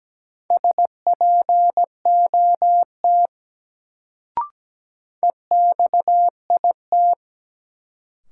Morse code, heard as a keyed tone, about 17 wpm, 700 Hz. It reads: SPOT EXIT
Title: Sentences from Top 1100 Words - New Word Drill Alt - 17wpm